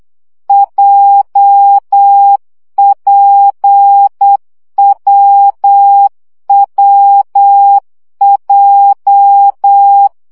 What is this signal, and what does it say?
Morse code, heard as a keyed tone, about 8 wpm, 800 Hz. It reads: JPWWJ
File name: CW gyakorlas w p j 7 baud